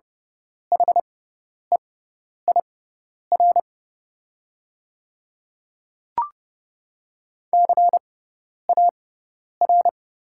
Morse code, heard as a keyed tone, about 30 wpm, spaced as for 10 wpm, 690 Hz. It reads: HEIR CAR
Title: 6x Character Spacing - Top 500 Words - 30wpm